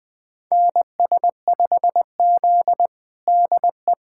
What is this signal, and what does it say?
Morse code, heard as a keyed tone, about 20 wpm, 700 Hz. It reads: NS5Z DE
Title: USA Callsign DE Callsign - 4 Characters - 20wpm